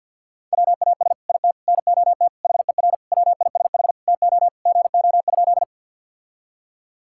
Morse code, heard as a keyed tone, about 50 wpm, 700 Hz. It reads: WAS IT NOT HER WISH TO GO?
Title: Sentences from Top 1600 Words 50wpm